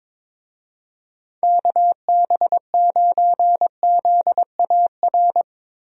Morse code, heard as a keyed tone, about 22 wpm, 700 Hz. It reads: KB9ZAR
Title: USA Call Signs - 6 Characters - Repeated 3x - 22wpm